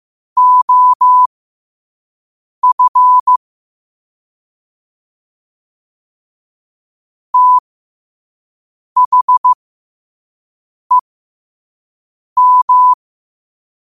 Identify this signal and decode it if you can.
Morse code, heard as a keyed tone, about 15 wpm, spaced as for 5 wpm, 1000 Hz. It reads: OF THEM
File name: qotdfiles / 5wpm